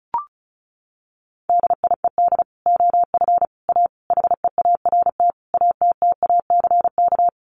Morse code, heard as a keyed tone, about 34 wpm, 700 Hz. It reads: DIED OF A HEART ATTACK
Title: Sets of 5 Words Encore2 35wpm